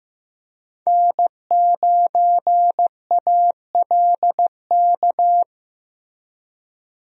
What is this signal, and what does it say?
Morse code, heard as a keyed tone, about 16 wpm, 700 Hz. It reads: N9ALK
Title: USA Call Signs - 5 Characters - Repeated 3x - 15wpm